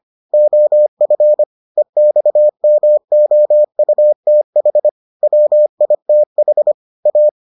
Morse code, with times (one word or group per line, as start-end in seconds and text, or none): 0.3-1.4 OF
1.8-4.9 EXMOUTH
5.2-6.7 WITH
7.1-7.3 A